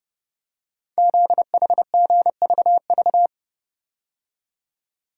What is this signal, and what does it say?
Morse code, heard as a keyed tone, about 30 wpm, 700 Hz. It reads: ZHGVV
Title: Code Groups - 5 Letters - 30wpm